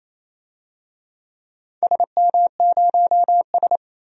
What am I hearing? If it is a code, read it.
Morse code, heard as a keyed tone, about 28 wpm, 700 Hz. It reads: SM0S
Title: DXCC-100 Call Signs - 4 Characters - 28wpm